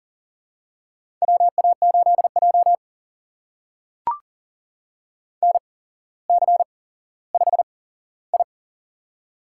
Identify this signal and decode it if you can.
Morse code, heard as a keyed tone, about 39 wpm, spaced as for 17 wpm, 700 Hz. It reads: WA8J NC5I
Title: USA Call Signs - 4 Characters - 8x Character Spacing - 40wpm